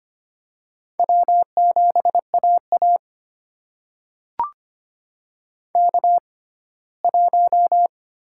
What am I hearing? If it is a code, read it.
Morse code, heard as a keyed tone, about 25 wpm, spaced as for 17 wpm, 700 Hz. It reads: W7AA K1
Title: USA Call Signs - 4 Characters - 6x Character Spacing - 25wpm